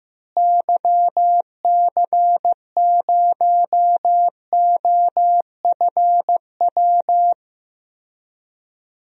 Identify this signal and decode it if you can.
Morse code, heard as a keyed tone, about 15 wpm, 700 Hz. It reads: YC0OFW
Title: DXCC-100 Call Signs - 6 Characters - Repeated 3x - 15wpm